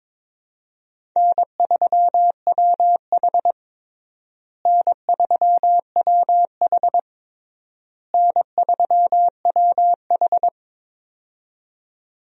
Morse code, heard as a keyed tone, about 22 wpm, 700 Hz. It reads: N3WH N3WH N3WH
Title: USA Call Signs - 4 Characters - Repeated 3x - 22wpm